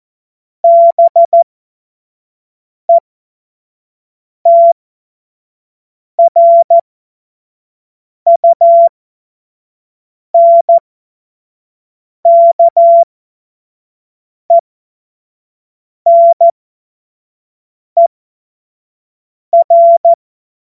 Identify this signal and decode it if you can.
Morse code, heard as a keyed tone, about 14 wpm, spaced as for 5 wpm, 680 Hz. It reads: BETRUNKENER
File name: Morsekurs / Witze / Witz 01. Juni 2021, 25 F70